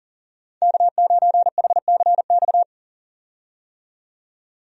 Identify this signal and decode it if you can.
Morse code, heard as a keyed tone, about 40 wpm, 700 Hz. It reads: K9HCX
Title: USA Call Signs - 5 Characters - 40wpm